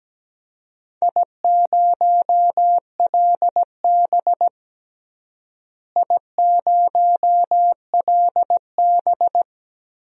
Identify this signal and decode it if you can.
Morse code, heard as a keyed tone, about 17 wpm, 700 Hz. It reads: I0LB I0LB
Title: DXCC-100 Call Signs - 4 Characters - Repeated 2x - 17wpm